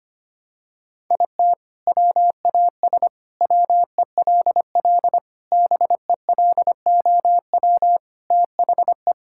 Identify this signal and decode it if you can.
Morse code, heard as a keyed tone, about 25 wpm, 700 Hz. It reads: IT WAS WELL BELOW THE
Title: Sentences from Top 1000 Words - Review Part 6 - 25wpm